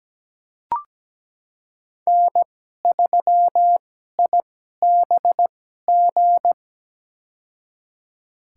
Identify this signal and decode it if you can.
Morse code, heard as a keyed tone, about 17 wpm, spaced as for 12 wpm, 700 Hz. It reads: N3IBG
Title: USA Call Signs - 5 Characters - 2x Character Spacing - 17wpm